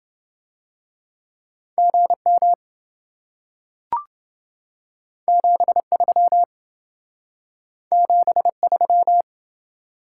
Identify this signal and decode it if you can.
Morse code, heard as a keyed tone, about 27 wpm, 700 Hz. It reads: GM 73 73